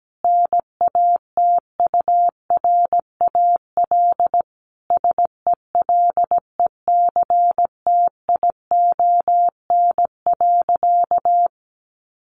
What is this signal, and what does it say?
Morse code, heard as a keyed tone, about 17 wpm, 700 Hz. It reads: NATURAL SELECTION.
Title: Sentences from Top 1400 Words 17wpm